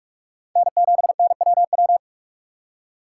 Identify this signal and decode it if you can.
Morse code, heard as a keyed tone, about 46 wpm, 700 Hz. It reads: N7NWW
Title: USA Call Signs - 5 Characters - 3x Character Spacing - 45wpm